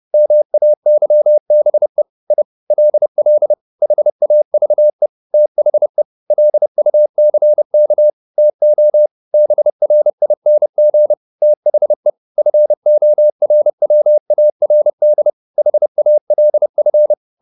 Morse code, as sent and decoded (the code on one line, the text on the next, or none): -- .- -.-- -... / .. / .-.. .-.. / .... .- ...- . / - .... / .-.. ..- -.-. -.- / - --- / -... .-. .. -. --. / - .... / ..-. --- .-. .-- .- .-. -.. / .... .- .-.. ..-.
MAYBE I LL HAVE THE LUCK TO BRING THE FORWARD HALF